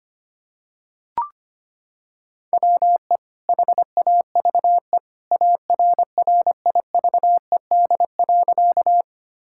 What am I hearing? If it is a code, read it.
Morse code, heard as a keyed tone, about 25 wpm, 700 Hz. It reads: WE HAVE ARRIVED.
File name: practice / 25wpm / Sentences with 2-Word Phrases - 25wpm